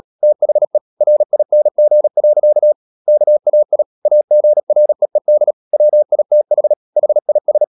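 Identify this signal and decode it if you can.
Morse code, heard as a keyed tone, about 37 wpm, 600 Hz. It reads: THE RING. KAI AGREED WITH HIS
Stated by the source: Morse code practice at 35–39 wpm sending English text of Ken Follett's novel Never, from CW med Gnister.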